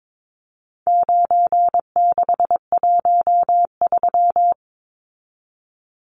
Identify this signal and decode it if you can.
Morse code, heard as a keyed tone, about 22 wpm, 700 Hz. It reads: 9613
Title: Numbers - 4-Digit 22wpm